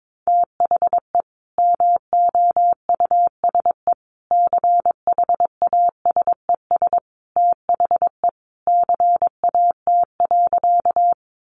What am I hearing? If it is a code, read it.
Morse code, heard as a keyed tone, about 22 wpm, 700 Hz. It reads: THE MOUSE CHASES THE CAT.
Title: Easy Phrases 22wpm